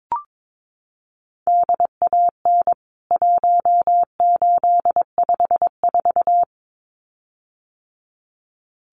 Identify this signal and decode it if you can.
Morse code, heard as a keyed tone, about 22 wpm, 700 Hz. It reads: DAN 1854
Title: CWT Exchange 22wpm